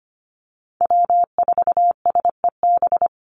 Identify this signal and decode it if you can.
Morse code, heard as a keyed tone, about 25 wpm, 700 Hz. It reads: W4SEB
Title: USA Call Signs 25wpm